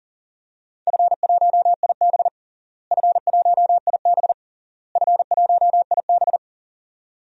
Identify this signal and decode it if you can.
Morse code, heard as a keyed tone, about 40 wpm, 700 Hz. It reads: F1IB F1IB F1IB
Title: DXCC-100 Call Signs - 4 Characters - Repeated 3x - 40wpm